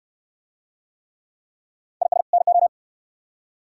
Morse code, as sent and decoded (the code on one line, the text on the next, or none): .. ... / -. ---
IS NO